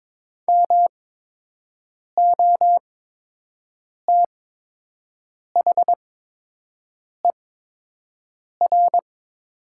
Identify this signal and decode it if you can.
Morse code, heard as a keyed tone, about 22 wpm, spaced as for 6 wpm, 700 Hz. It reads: MOTHER
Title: 8x Character Spacing - Top 200 Words - 22wpm